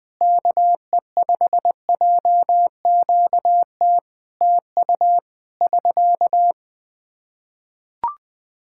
Morse code, heard as a keyed tone, about 21 wpm, 700 Hz. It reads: KE5JQT TU <SK>
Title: Practice QSO - Exchange 4 - Ending - 20wpm